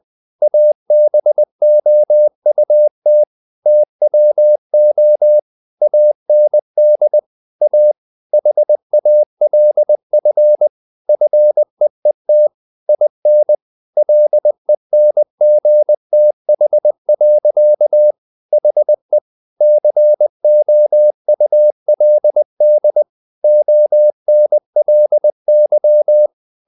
Morse code, heard as a keyed tone, about 20 wpm, 600 Hz. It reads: ABOUT TWO AND A HALF FEET IN LENGTH. HE COULD ONLY
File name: the old man and the sea part 2 20wpm